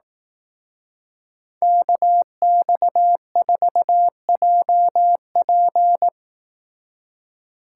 Morse code, heard as a keyed tone, about 18 wpm, 700 Hz. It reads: KX4JP
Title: Speed Racing - USA Call Signs 15wpm